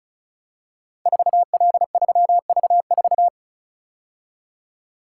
Morse code, heard as a keyed tone, about 36 wpm, 690 Hz. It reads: 4L3V4
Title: Code Groups - 5 Letter:Numbers - 35wpm